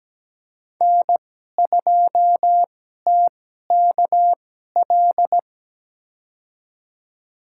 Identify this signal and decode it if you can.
Morse code, heard as a keyed tone, about 17 wpm, spaced as for 12 wpm, 700 Hz. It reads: N2TKL
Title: USA Call Signs - 5 Characters - 2x Character Spacing - 17wpm